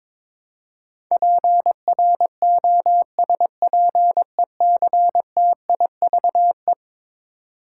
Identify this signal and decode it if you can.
Morse code, heard as a keyed tone, about 22 wpm, 700 Hz. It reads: PROSPECTIVE
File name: practice / 22wpm / Eleven Letter Words - 22wpm